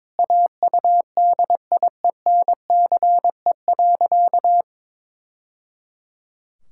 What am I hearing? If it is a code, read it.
Morse code, heard as a keyed tone, about 21 wpm, 700 Hz. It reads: AUDIENCE.
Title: Sentences from Top 1000 Words - Review Part 5 - 22wpm